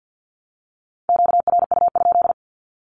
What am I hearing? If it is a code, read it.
Morse code, heard as a keyed tone, about 49 wpm, 700 Hz. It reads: QRU?
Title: Q-Codes 50wpm